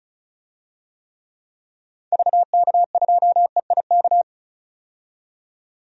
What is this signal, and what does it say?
Morse code, heard as a keyed tone, about 36 wpm, 700 Hz. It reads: VK2EIK
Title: DXCC-100 Call Signs - 6 Characters - 35wpm